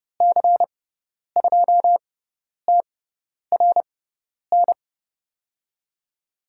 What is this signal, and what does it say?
Morse code, heard as a keyed tone, about 30 wpm, spaced as for 10 wpm, 700 Hz. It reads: C2TRN
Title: USA Call Signs - 6 Characters - 6x Character Spacing - 30wpm